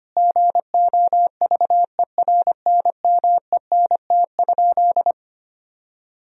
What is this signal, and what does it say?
Morse code, heard as a keyed tone, about 25 wpm, 700 Hz. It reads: GOVERNMENT?
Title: Sentences from Top 1000 Words - Review Part 4 - 25wpm